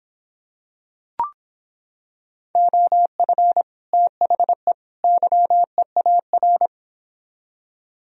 Morse code, heard as a keyed tone, about 26 wpm, 700 Hz. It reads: OF THE YEAR